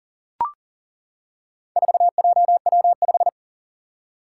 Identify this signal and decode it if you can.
Morse code, heard as a keyed tone, about 40 wpm, 700 Hz. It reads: 4JW5
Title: Four Challenging Letter-Number Combinations 40wpm